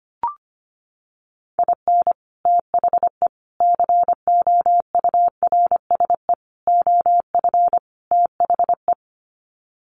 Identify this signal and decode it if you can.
Morse code, heard as a keyed tone, about 25 wpm, 700 Hz. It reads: IN THE COURSE OF THE
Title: Sets of 5 Words 25wpm